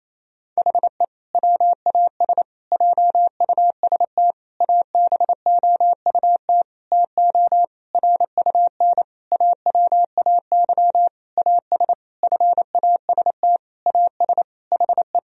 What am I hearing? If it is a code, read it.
Morse code, heard as a keyed tone, about 28 wpm, 700 Hz. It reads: HE WAS JUST ABOUT TO RUN AWAY AS FAST AS HE
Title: Sentences from Top 1000 Words - Review Part 2 - 28wpm